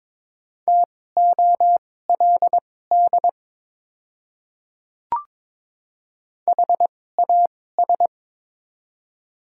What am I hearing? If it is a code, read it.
Morse code, heard as a keyed tone, about 22 wpm, spaced as for 16 wpm, 700 Hz. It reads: TOLD HAS
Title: 2x Character Spacing - Top 500 Words - 22wpm